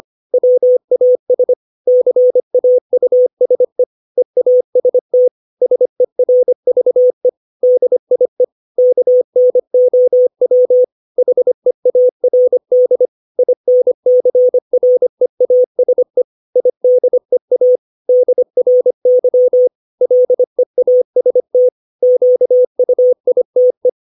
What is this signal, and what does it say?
Morse code, heard as a keyed tone, about 25 wpm, 500 Hz. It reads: WAS CAUSE EAST SERVE DIE KNOW HEARD INCREASE IDEA DRY LEAST QUITE
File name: 632 words / 632 Words 25WPM